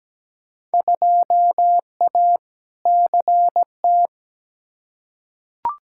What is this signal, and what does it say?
Morse code, heard as a keyed tone, about 17 wpm, 700 Hz. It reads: 2A CT
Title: Field Day Exchanges Encore - 17wpm